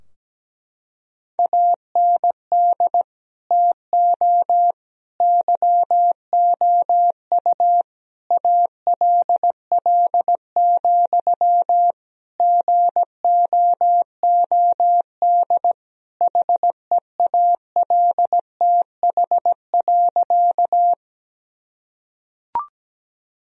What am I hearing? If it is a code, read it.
Morse code, heard as a keyed tone, about 17 wpm, 700 Hz. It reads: AND TO YOU ALL, GOOD HEALTH.
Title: Sentences from Top 1000 Words - Review Part 3 - 17wpm